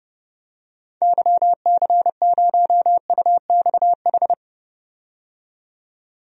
Morse code, heard as a keyed tone, about 31 wpm, 700 Hz. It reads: YC0UXH
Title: DXCC-100 Call Signs - 6 Characters - 30wpm